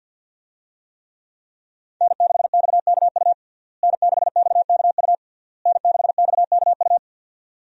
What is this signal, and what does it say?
Morse code, heard as a keyed tone, about 50 wpm, 700 Hz. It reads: N6XKU N6XKU N6XKU
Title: USA Call Signs - 5 Characters - Repeated 3x - 50wpm